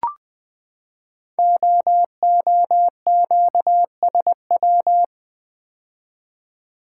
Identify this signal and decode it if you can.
Morse code, heard as a keyed tone, about 20 wpm, 700 Hz. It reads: OOQSW
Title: Code Groups - 5 Letters - 20wpm